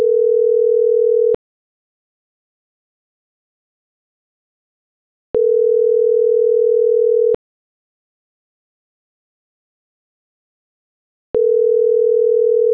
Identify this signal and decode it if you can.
Ringback tone (North America), 440 + 480 Hz.